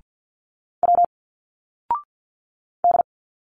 Morse code, heard as a keyed tone, about 51 wpm, 700 Hz. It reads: W D
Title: Single Letters 50wpm